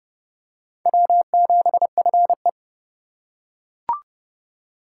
Morse code, heard as a keyed tone, about 26 wpm, 700 Hz. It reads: W7FE